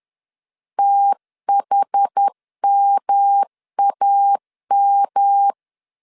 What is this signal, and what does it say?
Morse code, heard as a keyed tone, about 10 wpm, 800 Hz. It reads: THMAM
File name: CW / DET / 1 3 1500C S C 10wpm 3335